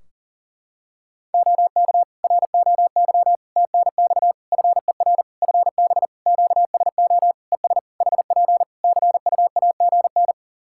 Morse code, heard as a keyed tone, about 37 wpm, 700 Hz. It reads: OK ROY TNX FER FB QSO ES HP CUAGN
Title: Practice QSO - Exchange 4 - Ending - 40wpm